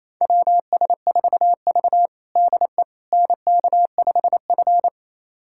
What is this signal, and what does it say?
Morse code, heard as a keyed tone, about 28 wpm, 700 Hz. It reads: WS4V DE NK5F
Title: USA Callsign DE Callsign - 4 Characters - 28wpm